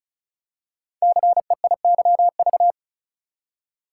Morse code, heard as a keyed tone, about 36 wpm, 700 Hz. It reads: CEIYV
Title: Code Groups - 5 Letter:Numbers - 35wpm